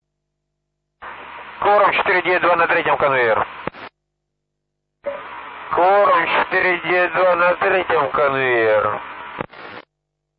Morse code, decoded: none

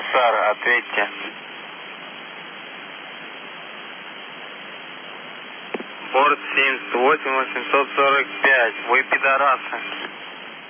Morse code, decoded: none

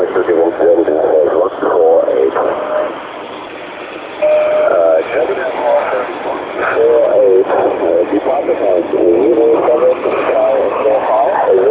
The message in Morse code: none

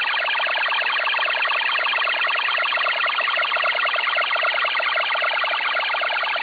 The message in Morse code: none

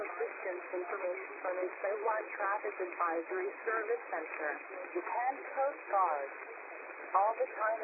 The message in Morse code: none

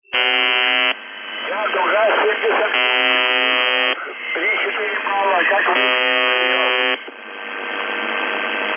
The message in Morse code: none

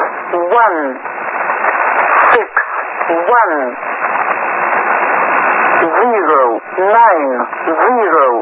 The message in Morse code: none

